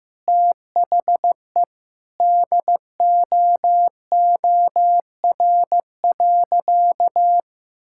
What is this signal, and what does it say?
Morse code, heard as a keyed tone, about 15 wpm, 700 Hz. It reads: THE DOOR.
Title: Sentences from Top 1600 Words 15wpm